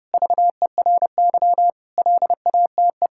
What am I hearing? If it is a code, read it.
Morse code, heard as a keyed tone, about 30 wpm, 700 Hz. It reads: VERY LATE
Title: Sentences from Top 1000 Words - Review Part 6 - 30wpm